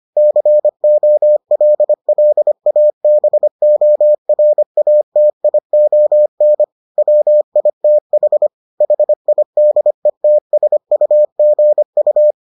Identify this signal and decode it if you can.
Morse code, heard as a keyed tone, about 25 wpm, 600 Hz. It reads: COLLABORATION WITH HIDETSUGU